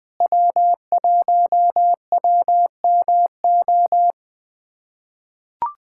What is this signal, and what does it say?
Morse code, heard as a keyed tone, about 20 wpm, 700 Hz. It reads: W1WMO
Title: USA Callsign DE Callsign - 5 Characters - 20wpm